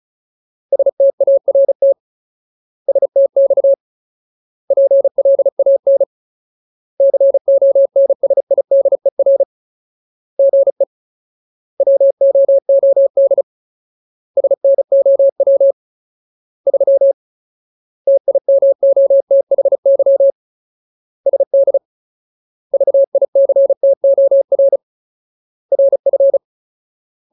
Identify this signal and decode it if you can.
Morse code, heard as a keyed tone, about 34 wpm, 550 Hz. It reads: START STX PLAN CONSIDER GE WOOD SNOW 3 TIMOTHY SD VICTOR RF
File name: Ultimate Word List / 35WPM Ultimate